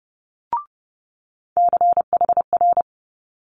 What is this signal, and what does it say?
Morse code, heard as a keyed tone, about 32 wpm, 700 Hz. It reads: CHR
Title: Multi-letter Phonemes Quick Fire 30wpm